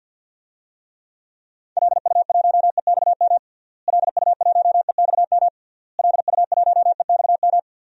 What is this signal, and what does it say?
Morse code, heard as a keyed tone, about 52 wpm, 700 Hz. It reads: LU1EXM LU1EXM LU1EXM
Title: DXCC-100 Call Signs - 6 Characters - Repeated 3x - 50wpm